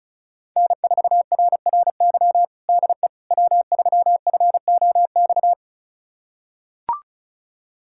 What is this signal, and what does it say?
Morse code, heard as a keyed tone, about 36 wpm, 700 Hz. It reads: N4RRY DE W3FOX
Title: USA Callsign DE Callsign - 5 Characters - 35wpm